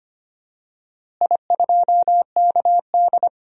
Morse code, heard as a keyed tone, about 25 wpm, 700 Hz. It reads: I2KD
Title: DXCC-100 Call Signs - 4 Characters - Repeated 2x - 25wpm